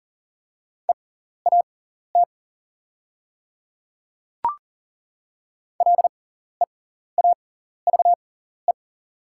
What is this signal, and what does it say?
Morse code, heard as a keyed tone, about 41 wpm, spaced as for 14 wpm, 700 Hz. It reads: EAT LEAVE
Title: 6x Character Spacing - Top 500 Words - 40wpm